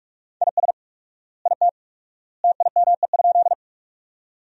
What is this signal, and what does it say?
Morse code, heard as a keyed tone, about 46 wpm, 700 Hz. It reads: IS IT TIME?
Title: 4x Word-Spacing - Sentences from Top 100 Words - 45wpm